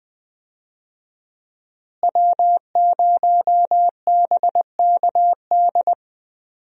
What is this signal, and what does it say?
Morse code, heard as a keyed tone, about 20 wpm, 700 Hz. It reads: W0BKD
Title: USA Call Signs - 5 Characters - Repeated 2x - 20wpm